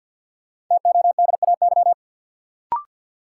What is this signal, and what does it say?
Morse code, heard as a keyed tone, about 52 wpm, 700 Hz. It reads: TODAY
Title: Sets of 5 Words Encore - New Word Drill - 50wpm